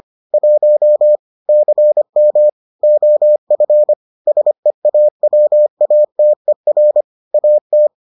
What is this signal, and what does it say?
Morse code, heard as a keyed tone, about 25 wpm, 600 Hz. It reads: 1 CM OF SEAWATER AT